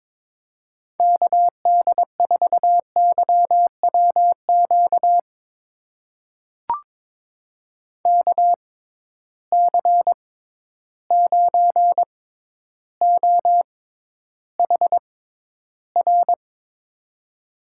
Morse code, heard as a keyed tone, about 22 wpm, spaced as for 11 wpm, 700 Hz. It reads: KD4YWQ KC9OHR
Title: USA Call Signs - 6 Characters - 6x Character Spacing - 22wpm